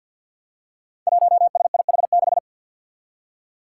Morse code, heard as a keyed tone, about 50 wpm, 700 Hz. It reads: 1SIH6
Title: Five Challenging Letter-Number Combinations 50wpm